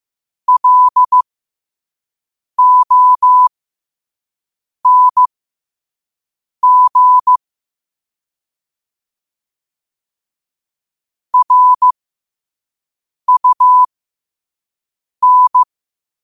Morse code, heard as a keyed tone, about 15 wpm, spaced as for 5 wpm, 1000 Hz. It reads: LONG RUN
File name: qotdfiles / 5wpm